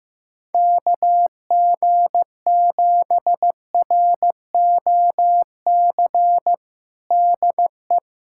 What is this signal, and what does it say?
Morse code, heard as a keyed tone, about 15 wpm, 700 Hz. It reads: KG7ROC DE
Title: USA Callsign DE Callsign - 6 Characters - 15wpm